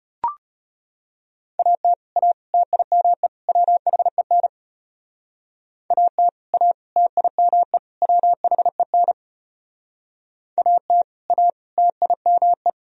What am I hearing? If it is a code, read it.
Morse code, heard as a keyed tone, about 32 wpm, 700 Hz. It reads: AT A TIME WHEN AT A TIME WHEN AT A TIME